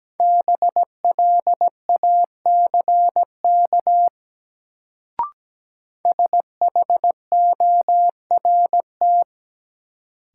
Morse code, heard as a keyed tone, about 17 wpm, 700 Hz. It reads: BLACK SHORT
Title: Sentences from Top 400 Words - New Word Drill - 17wpm